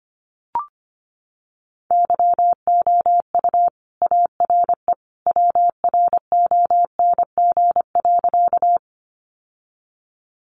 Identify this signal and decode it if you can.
Morse code, heard as a keyed tone, about 25 wpm, 700 Hz. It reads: YOU ARE WRONG.
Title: Sentences from Top 900 Words 25wpm